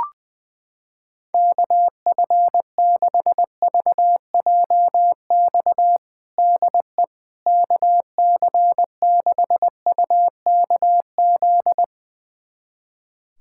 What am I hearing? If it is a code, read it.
Morse code, heard as a keyed tone, about 20 wpm, 700 Hz. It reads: KF6VJX DE KC6UKZ
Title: USA Callsign DE Callsign - 6 Characters - 20wpm